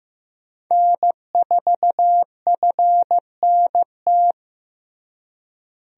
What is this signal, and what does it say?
Morse code, heard as a keyed tone, about 15 wpm, 700 Hz. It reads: N4FNT
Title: USA Call Signs - 5 Characters - Repeated 2x - 15wpm